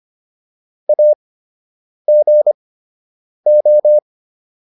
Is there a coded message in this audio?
Yes: Morse code from 0.9 to 4.0 s, heard as a keyed tone, about 25 wpm, spaced as for 8 wpm, 600 Hz. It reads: AGO